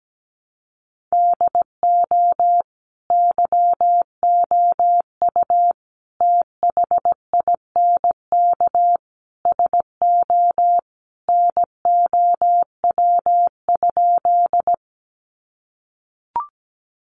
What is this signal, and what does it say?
Morse code, heard as a keyed tone, about 17 wpm, 700 Hz. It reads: DO YOU THINK SO NOW?
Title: Sentences from Top 200 Words 17wpm